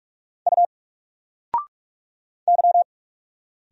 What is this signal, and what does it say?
Morse code, heard as a keyed tone, about 45 wpm, 700 Hz. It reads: U Y
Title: Single Letter-Number 45wpm